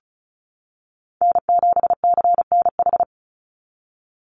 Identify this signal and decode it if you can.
Morse code, heard as a keyed tone, about 34 wpm, 700 Hz. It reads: N7CNH
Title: NA SOTA Chaser Callsigns 35wpm